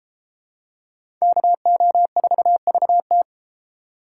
Morse code, heard as a keyed tone, about 31 wpm, 700 Hz. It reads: KO4VT